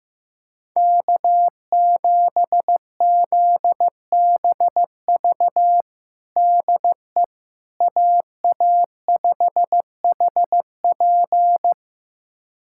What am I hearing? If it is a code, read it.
Morse code, heard as a keyed tone, about 16 wpm, 700 Hz. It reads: K7ZBV DE AA5HP
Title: USA Callsign DE Callsign - 5 Characters - 15wpm